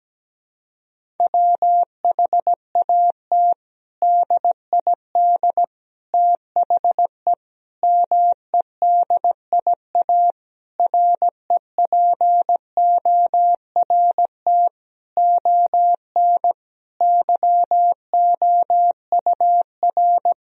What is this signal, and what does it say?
Morse code, heard as a keyed tone, about 17 wpm, 700 Hz. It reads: WHAT DID THE MEDIA REPORT ON YOUR
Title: Sentences from Top 1000 Words - Review Part 4 - 17wpm